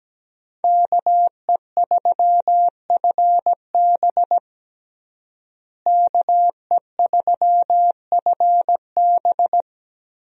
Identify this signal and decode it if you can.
Morse code, heard as a keyed tone, about 17 wpm, 700 Hz. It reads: KE3FB KE3FB
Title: USA Call Signs - 5 Characters - Repeated 3x - 17wpm